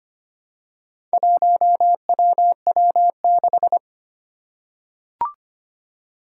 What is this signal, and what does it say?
Morse code, heard as a keyed tone, about 25 wpm, 700 Hz. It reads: 1WW6
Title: Four Challenging Letter-Number Combinations 25wpm